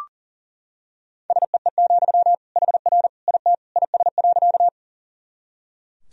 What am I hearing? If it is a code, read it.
Morse code, heard as a keyed tone, about 40 wpm, 700 Hz. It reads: SEE, HR IT IS.
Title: Sentences from Top 200 Words and Abbreviations 40wpm